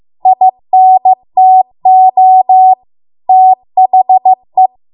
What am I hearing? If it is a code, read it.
Morse code, heard as a keyed tone, about 15 wpm, 750 Hz. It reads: INTO THE